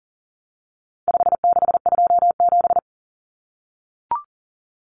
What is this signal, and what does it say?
Morse code, heard as a keyed tone, about 40 wpm, 700 Hz. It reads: 5627
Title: Common Cut Numbers - 4-Digit 40wpm